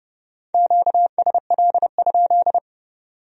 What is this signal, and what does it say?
Morse code, heard as a keyed tone, about 37 wpm, 700 Hz. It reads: QSL?
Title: Speed-Racing Q-Codes - 30wpm